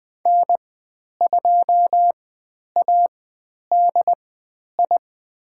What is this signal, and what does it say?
Morse code, heard as a keyed tone, about 20 wpm, spaced as for 10 wpm, 700 Hz. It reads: N2ADI
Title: 102 Lesson 17g - Letter u - Call Signs 20wpm